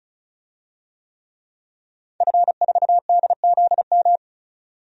Morse code, heard as a keyed tone, about 34 wpm, 700 Hz. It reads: F4DZM
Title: DXCC-100 Call Signs - 5 Characters - 35wpm